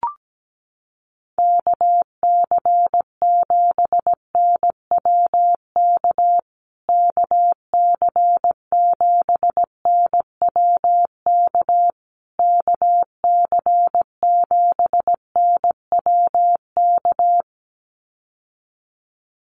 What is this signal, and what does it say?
Morse code, heard as a keyed tone, about 17 wpm, 700 Hz. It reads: KC7NWK KC7NWK KC7NWK
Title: USA Call Signs Repeated 3x 17wpm